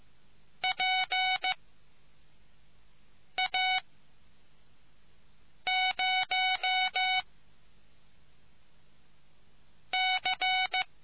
Morse code, heard as a keyed tone, about 15 wpm, 750 Hz. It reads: P A 0 C